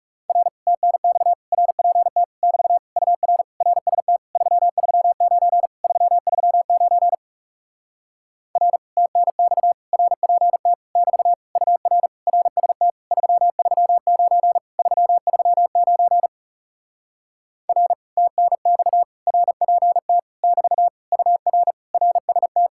Morse code, heard as a keyed tone, about 40 wpm, 700 Hz. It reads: R TNX RPT <BT> UR RST 339 339 R TNX RPT <BT> UR RST 339 339 R TNX RPT <BT> UR RST